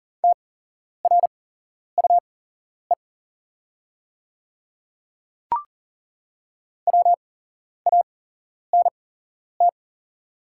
Morse code, heard as a keyed tone, about 40 wpm, spaced as for 11 wpm, 690 Hz. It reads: TRUE WANT
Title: 8x Character Spacing - Top 500 Words - 40wpm